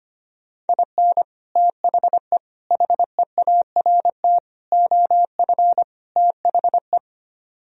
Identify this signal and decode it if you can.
Morse code, heard as a keyed tone, about 30 wpm, 700 Hz. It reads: IN THE HEART OF THE
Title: Speed-Racing - Sets of 5 Words 25wpm